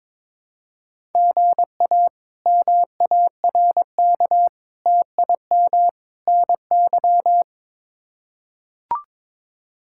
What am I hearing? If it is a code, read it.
Morse code, heard as a keyed tone, about 21 wpm, 700 Hz. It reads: GA MARK TIM NY
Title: K1USN SST QSO Practice - 22wpm